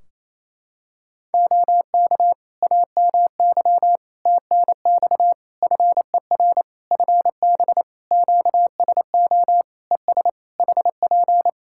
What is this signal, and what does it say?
Morse code, heard as a keyed tone, about 27 wpm, 700 Hz. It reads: OK AMY TNX FER FB QSO ES HP
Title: Practice QSO - Exchange 4 - Ending - 28wpm